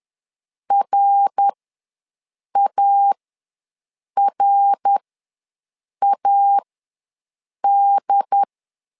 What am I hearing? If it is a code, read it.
Morse code, heard as a keyed tone, about 11 wpm, spaced as for 6 wpm, 800 Hz. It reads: RARAD
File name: LEZIONI / 3 MWRO / 3 MWR ws 10wpm 2151